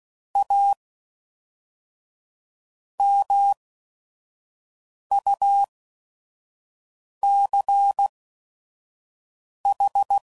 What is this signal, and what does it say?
Morse code, heard as a keyed tone, about 16 wpm, spaced as for 5 wpm, 800 Hz. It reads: A MUCH
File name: CW Code Groups @5WPM / Technical Subjects / Gates Intro